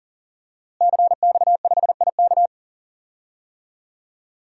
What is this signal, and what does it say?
Morse code, heard as a keyed tone, about 39 wpm, 700 Hz. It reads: CX5IK
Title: DXCC-100 Call Signs - 5 Characters - Repeated 2x - 40wpm